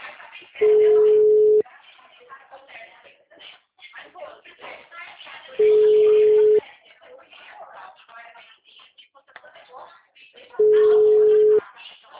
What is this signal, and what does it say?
Ringback tone (Europe, 425 Hz), with speech.